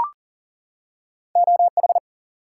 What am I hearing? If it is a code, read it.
Morse code, heard as a keyed tone, about 41 wpm, 700 Hz. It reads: O H
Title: US States and Canadian Provinces 40wpm